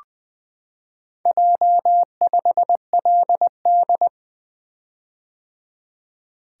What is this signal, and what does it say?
Morse code, heard as a keyed tone, about 20 wpm, 700 Hz. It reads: J5LD
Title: Four Letter-Number Combinations - 20wpm